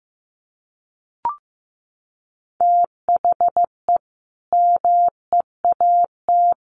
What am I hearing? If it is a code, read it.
Morse code, heard as a keyed tone, about 15 wpm, 700 Hz. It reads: THE MEAT
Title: A Sound Of Thunder 15wpm